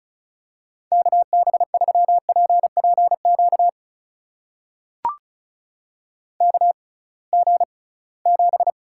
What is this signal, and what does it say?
Morse code, heard as a keyed tone, about 34 wpm, spaced as for 22 wpm, 700 Hz. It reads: KB3PPQ KG7
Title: USA Call Signs - 6 Characters - 6x Character Spacing - 35wpm